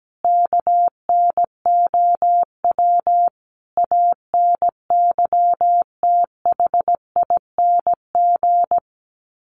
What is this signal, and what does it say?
Morse code, heard as a keyed tone, about 17 wpm, 700 Hz. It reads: KNOW ANYTHING
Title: Sets of 5 Words Encore 17wpm